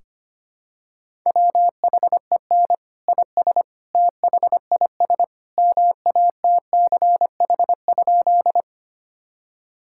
Morse code, heard as a keyed tone, about 25 wpm, 700 Hz. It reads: WHEN IS THIS MATCH?
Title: Sentences from Top 1600 Words 25wpm